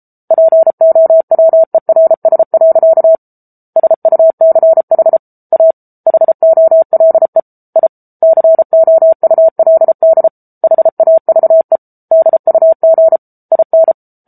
Morse code, heard as a keyed tone, about 34 wpm, 650 Hz. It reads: POWERS. SUCH A HOLE I COULD HAVE DUG IN
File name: Book Two / 07-Chapter 7 - 33 WPM